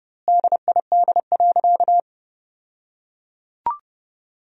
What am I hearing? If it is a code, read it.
Morse code, heard as a keyed tone, about 30 wpm, 700 Hz. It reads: DID.
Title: 2x Word-Spacing - Sentences from Top 100 Words - 30wpm